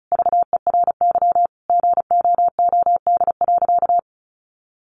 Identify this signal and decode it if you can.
Morse code, heard as a keyed tone, about 34 wpm, 700 Hz. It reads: VERY GOOD.
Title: Sentences from Top 1400 Words 35wpm